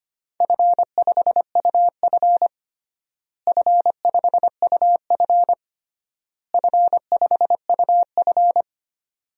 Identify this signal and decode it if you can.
Morse code, heard as a keyed tone, about 25 wpm, 700 Hz. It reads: F5UF F5UF F5UF
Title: DXCC-100 Call Signs - 4 Characters - Repeated 3x - 25wpm